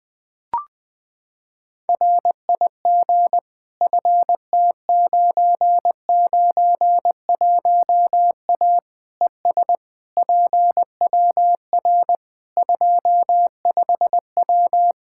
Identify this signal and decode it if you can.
Morse code, heard as a keyed tone, about 21 wpm, 700 Hz. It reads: RIG FT991A ES PWR 25W
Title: Practice QSO - Exchange 2 - Info - 20wpm